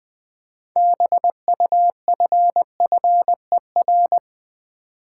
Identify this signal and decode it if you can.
Morse code, heard as a keyed tone, about 20 wpm, 700 Hz. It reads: BUFFER
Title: Build A Word - 2 Letters At A Time - 6-Letter Words - 20wpm